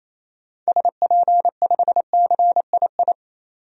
Morse code, heard as a keyed tone, about 28 wpm, 700 Hz. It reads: SP5CII
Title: DXCC-100 Call Signs - 6 Characters - 28wpm